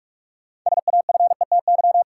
Morse code, heard as a keyed tone, about 46 wpm, 700 Hz. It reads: SAFETY
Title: Sentences from Top 700 Words - New Word Drill - 45wpm